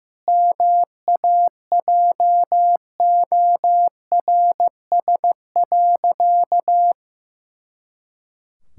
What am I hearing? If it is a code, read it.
Morse code, heard as a keyed tone, about 16 wpm, 700 Hz. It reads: MAJORS.
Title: Pangram Challenge v2 - 15wpm